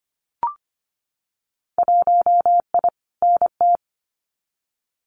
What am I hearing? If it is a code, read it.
Morse code, heard as a keyed tone, about 25 wpm, 700 Hz. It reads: 1I NT
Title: Winter Field Day Exchanges Encore 25wpm